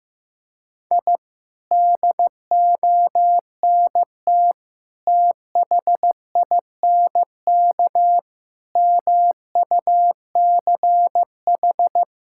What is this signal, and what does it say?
Morse code, heard as a keyed tone, about 15 wpm, 700 Hz. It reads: I DONT THINK MUCH
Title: Sentences from Top 1000 Words - Review Part 6 - 15wpm